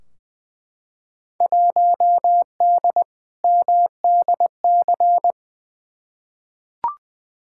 Morse code, heard as a keyed tone, about 21 wpm, 700 Hz. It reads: 1D MDC
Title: Field Day Exchanges - 20wpm